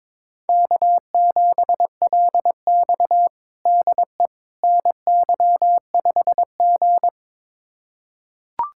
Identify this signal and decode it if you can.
Morse code, heard as a keyed tone, about 22 wpm, 700 Hz. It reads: K7LX DE NY5G
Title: USA Callsign DE Callsign - 4 Characters - 22wpm